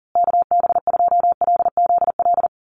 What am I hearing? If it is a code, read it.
Morse code, heard as a keyed tone, about 40 wpm, 700 Hz. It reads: KB2LZL
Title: USA Call Signs Repeated 3x 40wpm